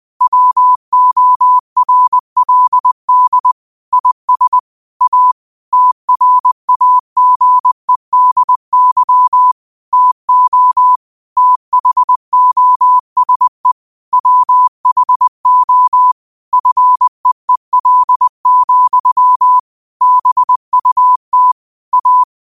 Morse code, heard as a keyed tone, about 20 wpm, 1000 Hz. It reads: WORLD IS A TRAGEDY TO THOSE WHO FEEL, BUT A
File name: qotdfiles / 20wpm